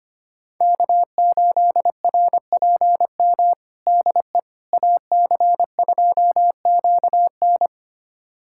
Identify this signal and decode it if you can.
Morse code, heard as a keyed tone, about 25 wpm, 700 Hz. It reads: K8RPM DE AC2QN
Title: USA Callsign DE Callsign - 5 Characters - 25wpm